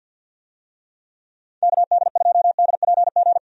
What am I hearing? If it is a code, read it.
Morse code, heard as a keyed tone, about 50 wpm, 700 Hz. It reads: KD2DPG
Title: USA Call Signs - 6 Characters - 50wpm